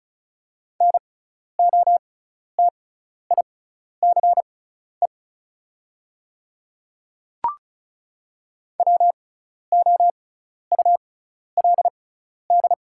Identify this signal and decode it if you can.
Morse code, heard as a keyed tone, about 36 wpm, spaced as for 12 wpm, 690 Hz. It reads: NOTICE WOULD
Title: 6x Character Spacing - Top 500 Words - 35wpm